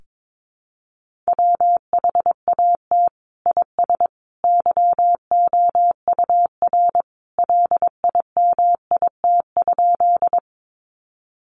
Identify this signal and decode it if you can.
Morse code, heard as a keyed tone, about 22 wpm, 700 Hz. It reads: WHAT IS YOUR LIMIT?
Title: Sentences from Top 1500 Words 22wpm